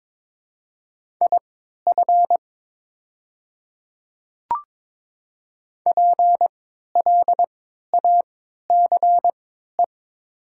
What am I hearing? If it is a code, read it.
Morse code, heard as a keyed tone, about 22 wpm, spaced as for 13 wpm, 700 Hz. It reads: IF PLACE
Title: 3x Character Spacing - Top 200 Words - 22wpm